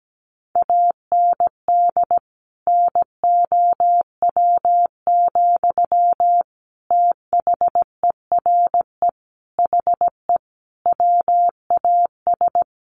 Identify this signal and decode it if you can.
Morse code, heard as a keyed tone, about 17 wpm, 700 Hz. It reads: AND NOW, THERE HE WAS
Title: Sentences from Top 1500 Words 17wpm